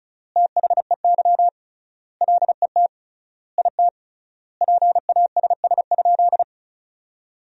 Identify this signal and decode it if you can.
Morse code, heard as a keyed tone, about 36 wpm, 700 Hz. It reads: THEY LET IT PASS?
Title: 3x Word-Spacing - Sentences from Top 400 Words - 35wpm